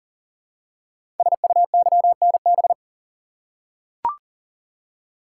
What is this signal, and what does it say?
Morse code, heard as a keyed tone, about 42 wpm, 700 Hz. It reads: SUYNB
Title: Code Groups - 5 Letters - 40wpm